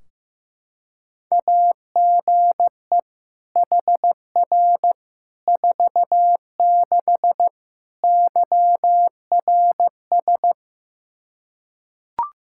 Morse code, heard as a keyed tone, about 15 wpm, 700 Hz. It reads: AGE HR 46 YRS
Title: Practice QSO - Exchange 3 - More Info - 15wpm